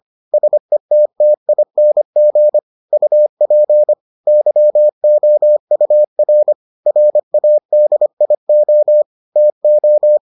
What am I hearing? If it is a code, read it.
Morse code, heard as a keyed tone, about 25 wpm, 600 Hz. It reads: SETTING UP YOUR RADIO TO